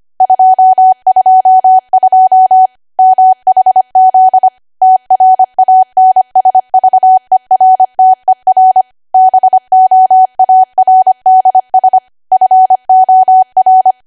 Morse code, heard as a keyed tone, about 25 wpm, 750 Hz. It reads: 222 MHZ TRANSVERTER BOARDS FOR
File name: Archive / 25 WPM / 25WPM